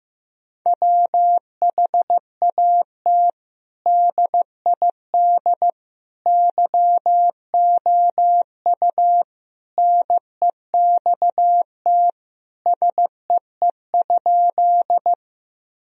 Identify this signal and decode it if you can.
Morse code, heard as a keyed tone, about 15 wpm, 700 Hz. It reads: WHAT DID YOU NEXT SEE?
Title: Sentences from Top 1000 Words - Review Part 1 - 15wpm